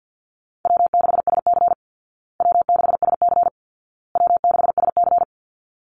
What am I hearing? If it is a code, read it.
Morse code, heard as a keyed tone, about 50 wpm, 700 Hz. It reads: W6SC W6SC W6SC
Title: Call Signs Repeated 2x - 50wpm